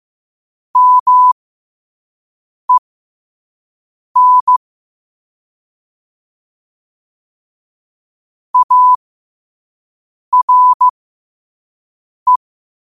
Morse code, heard as a keyed tone, about 15 wpm, spaced as for 5 wpm, 1000 Hz. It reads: MEN ARE
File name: 5wpm